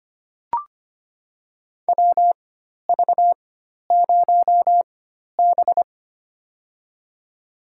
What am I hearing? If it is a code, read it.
Morse code, heard as a keyed tone, about 25 wpm, spaced as for 12 wpm, 700 Hz. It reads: WV0B
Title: USA Call Signs - 4 Characters - 4x Character Spacing - 25wpm